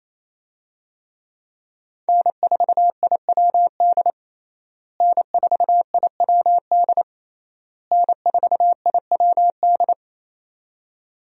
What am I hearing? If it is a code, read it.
Morse code, heard as a keyed tone, about 28 wpm, 700 Hz. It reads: N4IWD N4IWD N4IWD
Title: USA Call Signs - 5 Characters - Repeated 3x - 28wpm